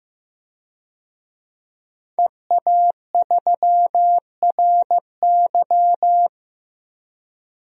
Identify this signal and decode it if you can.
Morse code, heard as a keyed tone, about 15 wpm, 700 Hz. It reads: EA3RY
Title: DXCC-100 Call Signs - 5 Characters - 15wpm